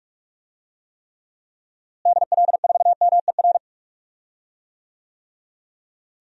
Morse code, heard as a keyed tone, about 46 wpm, 700 Hz. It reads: DL4MER
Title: DXCC-100 Call Signs - 6 Characters - Repeated 2x - 45wpm